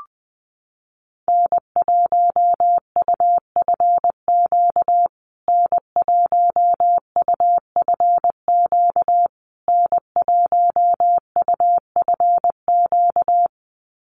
Morse code, heard as a keyed tone, about 20 wpm, 700 Hz. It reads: N1UFQ N1UFQ N1UFQ
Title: USA Call Signs Repeated 3x 20wpm